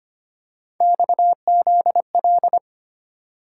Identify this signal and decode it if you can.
Morse code, heard as a keyed tone, about 29 wpm, 700 Hz. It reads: XZL
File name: Speed-Racing - Three Letter-Number Combinations - 25wpm